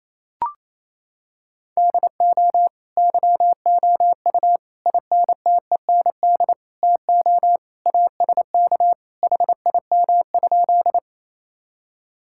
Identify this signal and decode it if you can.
Morse code, heard as a keyed tone, about 28 wpm, 700 Hz. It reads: DO YOU INTEND TO ASK HIM?
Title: Sentences from Top 1600 Words 28wpm